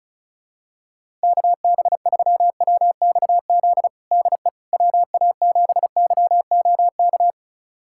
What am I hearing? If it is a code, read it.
Morse code, heard as a keyed tone, about 34 wpm, 700 Hz. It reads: KB3WXZ DE WA7YOK
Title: USA Callsign DE Callsign - 6 Characters - 35wpm